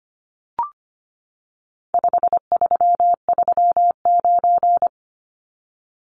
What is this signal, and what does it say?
Morse code, heard as a keyed tone, about 25 wpm, 700 Hz. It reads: 5339
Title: Numbers - 4-Digit 25wpm